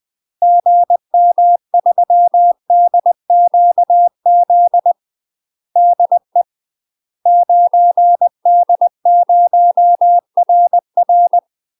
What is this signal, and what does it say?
Morse code, heard as a keyed tone, about 20 wpm, 700 Hz. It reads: GM3DQZ DE 9D0RR
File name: UK MASTER QSOs TRAINING / 20WPM / UK MASTER QSOs 0028